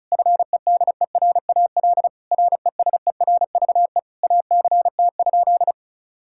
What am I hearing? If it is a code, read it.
Morse code, heard as a keyed tone, about 36 wpm, 700 Hz. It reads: FEDERAL RESERVE ACT?
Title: Sentences from Top 1000 Words - Review Part 6 - 35wpm